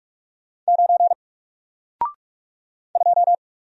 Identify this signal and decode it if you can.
Morse code, heard as a keyed tone, about 46 wpm, 700 Hz. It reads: 9 2
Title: Single Letter-Number Alt 45wpm